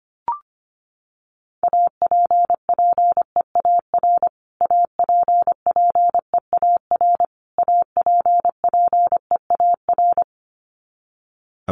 Morse code, heard as a keyed tone, about 25 wpm, 700 Hz. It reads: APPEAR APPEAR APPEAR
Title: Top 500 Words Repeated 25wpm